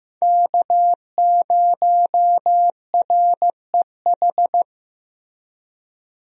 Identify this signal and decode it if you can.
Morse code, heard as a keyed tone, about 15 wpm, 700 Hz. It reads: K0REH
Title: USA Call Signs - 5 Characters - Repeated 2x - 15wpm